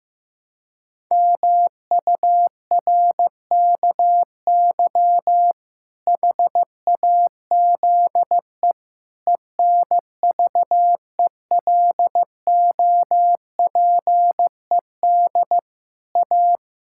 Morse code, heard as a keyed tone, about 16 wpm, 700 Hz. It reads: MURKY HAZE ENVELOPED A
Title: Pangram Challenge v2 - 15wpm